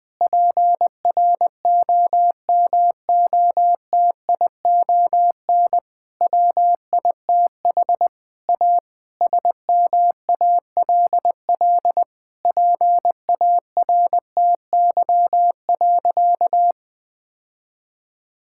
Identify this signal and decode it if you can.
Morse code, heard as a keyed tone, about 20 wpm, 700 Hz. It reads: PROMOTION WITH A SMALL PARTY.
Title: Difficult Sentences with Hints - 20wpm